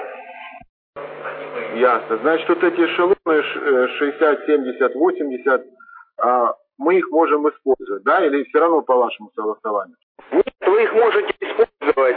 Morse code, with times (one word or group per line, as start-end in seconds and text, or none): none